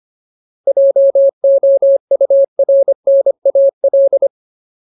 Morse code, heard as a keyed tone, about 25 wpm, 550 Hz. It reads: JOURNAL
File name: Ultimate Word List / 25WPM Ultimate